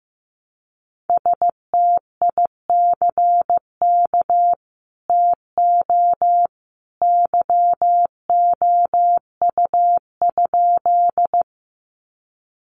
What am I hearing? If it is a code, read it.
Morse code, heard as a keyed tone, about 15 wpm, 700 Hz. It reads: STICK TO YOU?
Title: Sentences from Top 1200 Words 15wpm